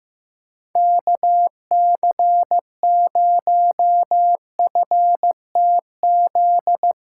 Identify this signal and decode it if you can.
Morse code, heard as a keyed tone, about 15 wpm, 700 Hz. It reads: KC0FTZ
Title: USA Call Signs - 6 Characters - 6x Character Spacing - 15wpm